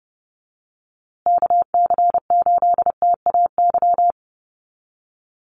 Morse code, heard as a keyed tone, about 30 wpm, 700 Hz. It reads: KC8TAY
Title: USA Call Signs Repeated 2x 30wpm